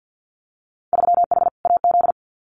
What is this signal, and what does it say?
Morse code, heard as a keyed tone, about 50 wpm, 700 Hz. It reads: 3H AZ
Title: Winter Field Day Exchanges Encore 50wpm